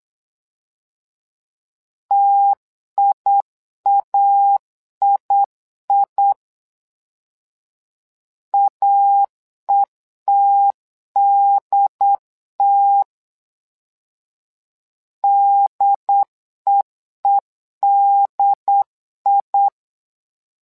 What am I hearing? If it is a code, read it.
Morse code, heard as a keyed tone, about 8 wpm, 800 Hz. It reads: TIAII AETDT DEEDI
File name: LEZIONI / 2 UHDN / 2 UHD 8wpm 2258